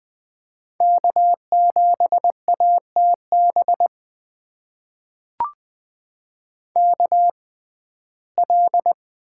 Morse code, heard as a keyed tone, about 20 wpm, spaced as for 15 wpm, 700 Hz. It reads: K7ATB KL
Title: USA Call Signs - 5 Characters - 6x Character Spacing - 20wpm